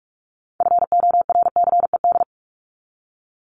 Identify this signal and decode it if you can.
Morse code, heard as a keyed tone, about 44 wpm, 700 Hz. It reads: FORCED
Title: ED Suffix - Top 100 Words with Base Words 45wpm